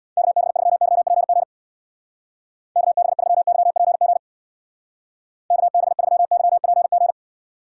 Morse code, heard as a keyed tone, about 61 wpm, 700 Hz. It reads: KB3YPG KB3YPG KB3YPG
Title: Speed Racing - USA Call Signs 50wpm